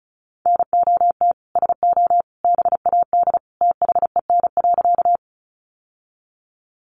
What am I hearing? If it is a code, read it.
Morse code, heard as a keyed tone, about 34 wpm, 700 Hz. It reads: NOT SO BAD THEN.
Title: Sentences from Top 700 Words 35wpm